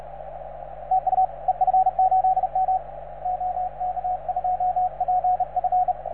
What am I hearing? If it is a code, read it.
Morse code, heard as a keyed tone, about 36 wpm, 750 Hz, with radio noise. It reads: TU EF8M OK2PF